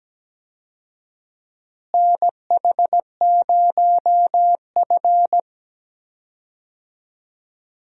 Morse code, heard as a keyed tone, about 17 wpm, 700 Hz. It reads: NH0F
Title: USA Call Signs - 4 Characters - Repeated 2x - 17wpm